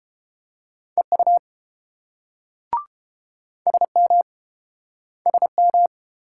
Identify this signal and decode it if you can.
Morse code, heard as a keyed tone, about 29 wpm, 700 Hz. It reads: EU SM SM